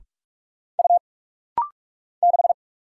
Morse code, heard as a keyed tone, about 44 wpm, 700 Hz. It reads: U 6
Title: Single Letter-Number Alt - Rapid-Fire 45wpm